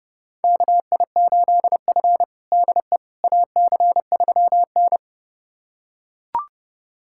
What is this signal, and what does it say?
Morse code, heard as a keyed tone, about 30 wpm, 700 Hz. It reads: KI8F DE AC3N
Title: USA Callsign DE Callsign - 4 Characters - 30wpm